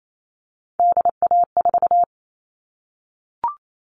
Morse code, heard as a keyed tone, about 28 wpm, 700 Hz. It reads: DA4
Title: All Cut Numbers - 3-Digit 28wpm